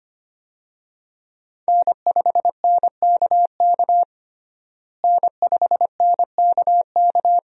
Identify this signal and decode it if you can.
Morse code, heard as a keyed tone, about 25 wpm, 700 Hz. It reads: N5NKK N5NKK
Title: USA Call Signs - 5 Characters - Repeated 2x - 25wpm